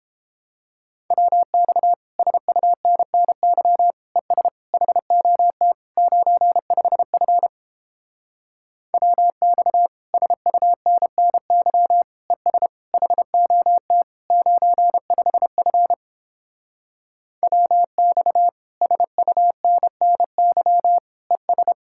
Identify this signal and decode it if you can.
Morse code, heard as a keyed tone, about 31 wpm, 700 Hz. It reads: WX SUNNY ES HOT 95F WX SUNNY ES HOT 95F WX SUNNY ES